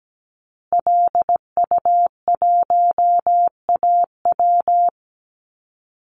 Morse code, heard as a keyed tone, about 17 wpm, 700 Hz. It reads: LU1AW
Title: Call Signs Repeated 2x - 17wpm